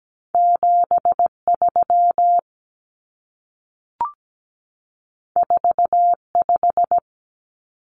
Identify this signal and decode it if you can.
Morse code, heard as a keyed tone, about 17 wpm, 700 Hz. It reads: 73 45
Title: Numbers - 2-Digit 17wpm